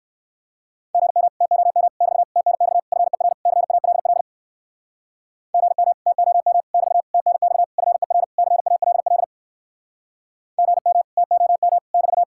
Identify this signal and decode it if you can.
Morse code, heard as a keyed tone, about 64 wpm, 700 Hz. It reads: GM TOM <BT> TNX FER CALL GM TOM <BT> TNX FER CALL GM TOM <BT>